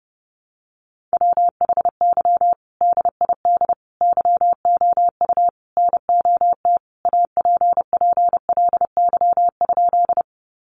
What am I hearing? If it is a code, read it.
Morse code, heard as a keyed tone, about 30 wpm, 700 Hz. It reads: WHY DID YOU NOT APPLY?
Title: Sentences from Top 1200 Words 30wpm